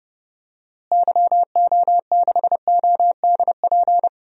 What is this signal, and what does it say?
Morse code, heard as a keyed tone, about 30 wpm, 700 Hz. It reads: YO6ODP
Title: DXCC-100 Call Signs - 6 Characters - Repeated 2x - 30wpm